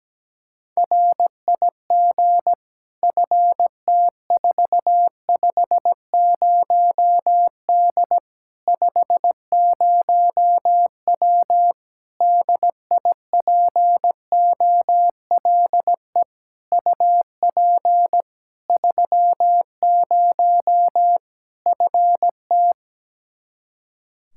Morse code, heard as a keyed tone, about 17 wpm, 700 Hz. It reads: RIG FT450D 50W DIPOLE UP 30 FT
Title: Practice QSO - Exchange 2 - Info - 17wpm